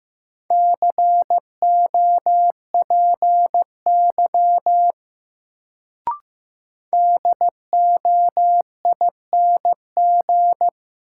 Morse code, heard as a keyed tone, about 15 wpm, 700 Hz. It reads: COPY DOING
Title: Sentences from Top 1400 Words - New Word Drill Alt - 15wpm